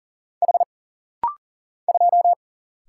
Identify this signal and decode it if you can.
Morse code, heard as a keyed tone, about 40 wpm, 700 Hz. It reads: H 2
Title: Single Letter-Number - Rapid-Fire 40wpm